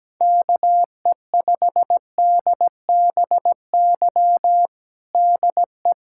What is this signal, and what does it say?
Morse code, heard as a keyed tone, about 17 wpm, 700 Hz. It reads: KE5DBY DE
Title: USA Callsign DE Callsign - 6 Characters - 17wpm